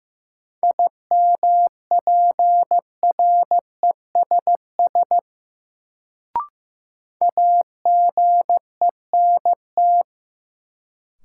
Morse code, heard as a keyed tone, about 15 wpm, 700 Hz. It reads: IMPRESS AGENT
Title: Sentences from Top 1600 Words - New Word Drill - 15wpm